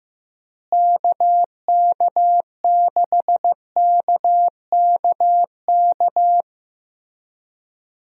Morse code, heard as a keyed tone, about 15 wpm, 700 Hz. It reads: KK6KKK
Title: USA Call Signs - 6 Characters - Repeated 3x - 15wpm